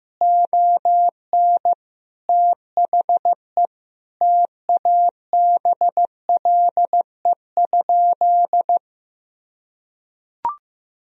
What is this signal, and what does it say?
Morse code, heard as a keyed tone, about 15 wpm, 700 Hz. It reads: ON THE TABLE?
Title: Sentences from Top 1000 Words - Review Part 3 - 15wpm